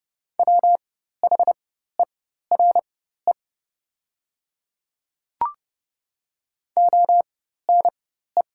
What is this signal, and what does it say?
Morse code, heard as a keyed tone, about 30 wpm, spaced as for 14 wpm, 700 Hz. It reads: WHERE ONE
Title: 4x Character Spacing - Top 200 Words - 30wpm